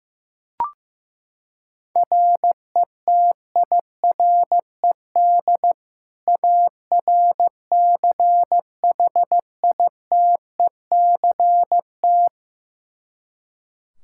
Morse code, heard as a keyed tone, about 15 wpm, 700 Hz. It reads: RETIRED ARCHITECT
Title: Practice QSO - Exchange 3 - More Info - 15wpm